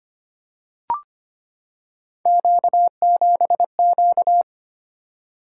Morse code, heard as a keyed tone, about 25 wpm, 700 Hz. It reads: Q7Q
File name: Three Letter-Number Combinations - 25wpm